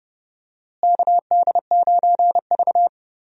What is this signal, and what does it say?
Morse code, heard as a keyed tone, about 32 wpm, 700 Hz. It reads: KD9V
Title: USA Call Signs - 4 Characters - Repeated 2x - 30wpm